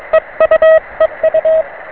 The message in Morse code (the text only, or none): EU EU